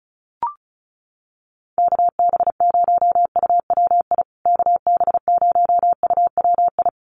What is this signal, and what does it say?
Morse code, heard as a keyed tone, about 34 wpm, 700 Hz. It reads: KB0UWI KB0UWI
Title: USA Call Signs Repeated 2x 35wpm